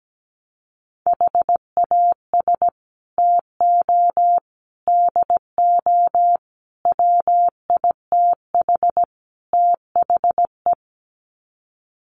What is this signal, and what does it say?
Morse code, heard as a keyed tone, about 17 wpm, 700 Hz. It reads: HAS TO DO WITH THE
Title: Sets of 5 Words 17wpm